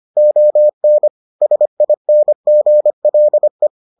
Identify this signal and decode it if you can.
Morse code, heard as a keyed tone, about 25 wpm, 600 Hz. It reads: ON SINGLE